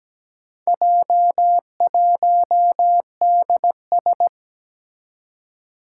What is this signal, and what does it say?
Morse code, heard as a keyed tone, about 17 wpm, 700 Hz. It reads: J1DS
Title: Four Challenging Letter-Number Combinations 17wpm